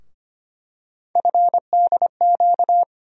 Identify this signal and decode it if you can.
Morse code, heard as a keyed tone, about 25 wpm, 700 Hz. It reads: FDQ
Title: USA Call Sign Postfixes - 3 Letters 25wpm